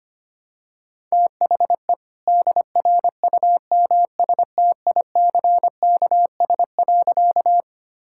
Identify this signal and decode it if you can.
Morse code, heard as a keyed tone, about 25 wpm, 700 Hz. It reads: THE DRUMSTICKS.
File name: practice / 25wpm / All-Age Appropriate Jokes - 25wpm